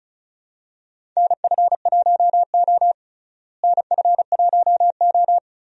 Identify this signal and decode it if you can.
Morse code, heard as a keyed tone, about 36 wpm, 700 Hz. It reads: NF1O NF1O
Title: USA Call Signs - 4 Characters - Repeated 2x - 35wpm